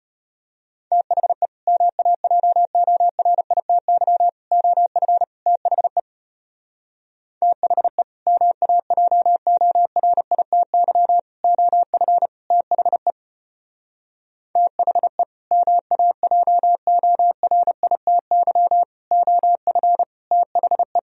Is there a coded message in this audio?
Yes: Morse code from 0.9 to 21.0 s, heard as a keyed tone, about 34 wpm, 700 Hz. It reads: THE MAJORITY OF THE THE MAJORITY OF THE THE MAJORITY OF THE